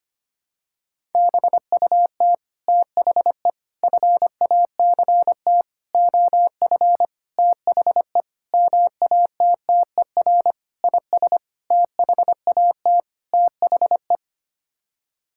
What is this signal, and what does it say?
Morse code, heard as a keyed tone, about 25 wpm, 700 Hz. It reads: BUT THE FACT OF THE MATTER IS THAT THE
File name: Sets of 9 Words - 25wpm